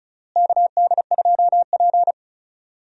Morse code, heard as a keyed tone, about 36 wpm, 700 Hz. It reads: KD2P
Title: USA Call Signs - 4 Characters - Repeated 3x - 35wpm